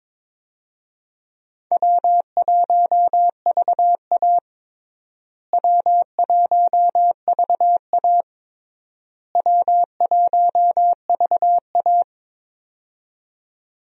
Morse code, heard as a keyed tone, about 22 wpm, 700 Hz. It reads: W1VA W1VA W1VA
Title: USA Call Signs - 4 Characters - Repeated 3x - 22wpm